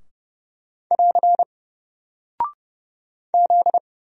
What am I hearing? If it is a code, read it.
Morse code, heard as a keyed tone, about 31 wpm, 700 Hz. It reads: <AR> Z
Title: Single Character (Letter:Number:Punctuation:Prosign) Alt 30wpm